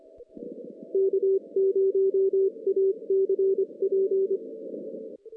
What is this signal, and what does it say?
Morse code, heard as a keyed tone, about 26 wpm, 390 Hz. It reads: K0ACP
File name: 2016 JAN CW NAQP / 2016 Jan naqp k5ot - 0-10